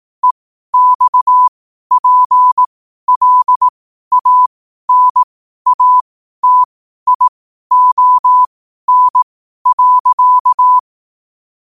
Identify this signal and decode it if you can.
Morse code, heard as a keyed tone, about 18 wpm, spaced as for 12 wpm, 1000 Hz. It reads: EXPLANATION.
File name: qotdfiles / 12wpm